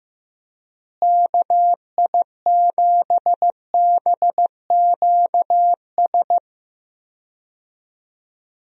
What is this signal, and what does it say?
Morse code, heard as a keyed tone, about 15 wpm, 700 Hz. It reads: KI7BQS
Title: USA Call Signs - 6 Characters - Repeated 3x - 15wpm